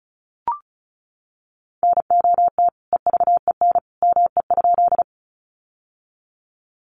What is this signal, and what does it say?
Morse code, heard as a keyed tone, about 34 wpm, 700 Hz. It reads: NOT EVEN ME?
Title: Sentences from Top 600 Words 35wpm